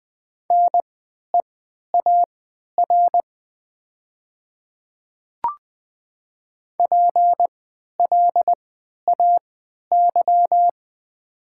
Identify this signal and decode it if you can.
Morse code, heard as a keyed tone, about 20 wpm, spaced as for 11 wpm, 700 Hz. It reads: NEAR PLAY
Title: 3x Character Spacing - Top 200 Words - 20wpm